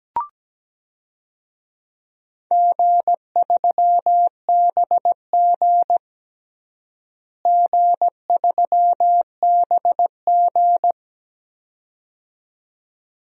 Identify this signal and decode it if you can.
Morse code, heard as a keyed tone, about 17 wpm, 700 Hz. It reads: G3BG G3BG
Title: DXCC-100 Call Signs - 4 Characters - Repeated 2x - 17wpm